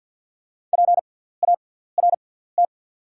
Morse code, heard as a keyed tone, about 48 wpm, spaced as for 17 wpm, 700 Hz. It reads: PART
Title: 6x Character Spacing - Top 200 Words - 50wpm